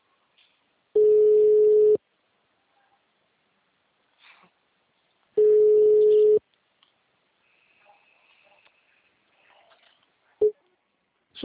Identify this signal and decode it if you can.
Ringback tone (Europe, 425 Hz), with speech.
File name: out- -2003- - - .778